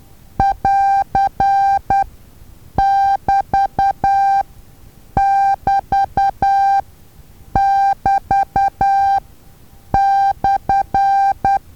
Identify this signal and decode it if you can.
Morse code, heard as a keyed tone, about 10 wpm, spaced as for 7 wpm, 790 Hz. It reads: <AR><BT><BT><BT>/
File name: Taviro HA6VA A